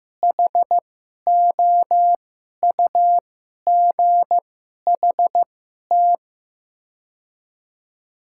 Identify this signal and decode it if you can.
Morse code, heard as a keyed tone, about 15 wpm, spaced as for 11 wpm, 700 Hz. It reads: HOUGHT
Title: 2x Character Spacing - Top 300 Words - 15wpm